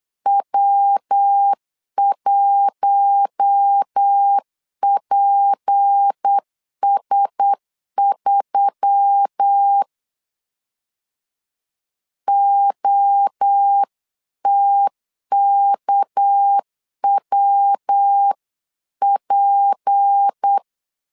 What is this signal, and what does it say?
Morse code, heard as a keyed tone, about 8 wpm, 800 Hz. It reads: W1PS3 OTKWP
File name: 8wpm